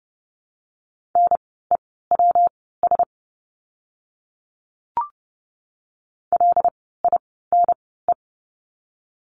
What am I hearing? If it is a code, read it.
Morse code, heard as a keyed tone, about 28 wpm, spaced as for 17 wpm, 700 Hz. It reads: NEWS LINE
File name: course / 30:10wpm / 115 Lesson 19f - Letter w - Words 30wpm